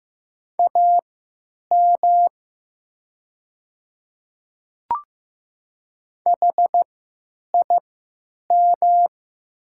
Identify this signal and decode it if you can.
Morse code, heard as a keyed tone, about 15 wpm, spaced as for 9 wpm, 700 Hz. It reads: AM HIM
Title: 3x Character Spacing - Top 500 Words - 15wpm